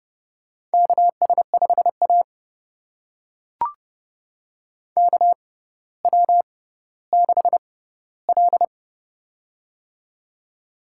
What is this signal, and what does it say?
Morse code, heard as a keyed tone, about 28 wpm, spaced as for 15 wpm, 690 Hz. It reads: KS5A KW6L
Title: USA Call Signs - 4 Characters - 6x Character Spacing - 30wpm